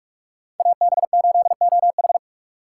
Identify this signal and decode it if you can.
Morse code, heard as a keyed tone, about 56 wpm, 700 Hz. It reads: AB8OH
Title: Speed Racing - USA 2x2 Call Signs 45wpm